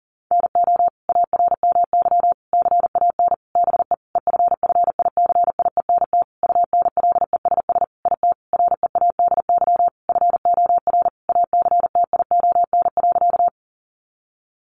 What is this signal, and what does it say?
Morse code, heard as a keyed tone, about 40 wpm, 700 Hz. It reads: NO ARMY CAN BE EFFICIENT UNLESS IT READY FOR ACTION.
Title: Sentences from Top 1300 Words 40wpm